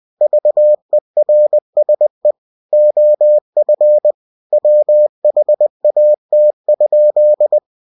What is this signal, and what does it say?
Morse code, heard as a keyed tone, about 20 wpm, 600 Hz. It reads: VERSE OF WHAT?
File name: winnie-20wpm